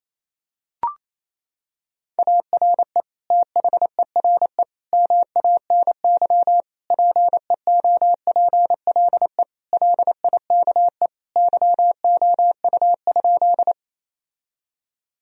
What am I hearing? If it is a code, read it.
Morse code, heard as a keyed tone, about 28 wpm, 700 Hz. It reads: ARE THERE MANY PEOPLE LIKE YOU?
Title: Sentences from Top 1000 Words - Review Part 1 - 28wpm